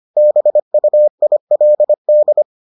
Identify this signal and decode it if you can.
Morse code, heard as a keyed tone, about 25 wpm, 600 Hz. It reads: BUILD